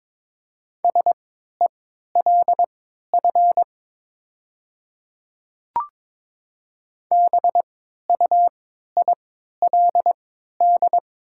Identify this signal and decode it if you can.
Morse code, heard as a keyed tone, about 21 wpm, spaced as for 12 wpm, 700 Hz. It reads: SELF BUILD
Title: 3x Character Spacing - Top 200 Words - 22wpm